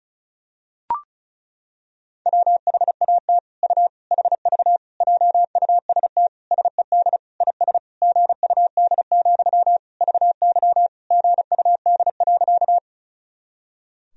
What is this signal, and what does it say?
Morse code, heard as a keyed tone, about 34 wpm, 700 Hz. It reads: WHAT U HV JUST SED IS GUD, VY GUD.
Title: Sentences from Top 200 Words and Abbreviations 35wpm